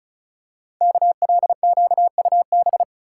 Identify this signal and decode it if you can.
Morse code, heard as a keyed tone, about 36 wpm, 690 Hz. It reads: KLQUB
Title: Code Groups - 5 Letter:Numbers - 35wpm